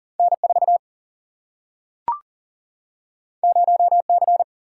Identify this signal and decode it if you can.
Morse code, heard as a keyed tone, about 40 wpm, 700 Hz. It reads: N4 0C
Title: Single-Letter Single-Number 40wpm